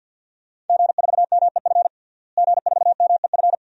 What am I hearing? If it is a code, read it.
Morse code, heard as a keyed tone, about 52 wpm, 700 Hz. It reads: G4MEF G4MEF
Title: DXCC-100 Call Signs - 5 Characters - Repeated 2x - 50wpm